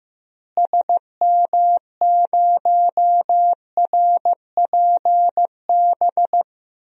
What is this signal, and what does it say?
Morse code, heard as a keyed tone, about 14 wpm, 700 Hz. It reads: SM0RPB
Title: DXCC-100 Call Signs - 6 Characters - Repeated 3x - 15wpm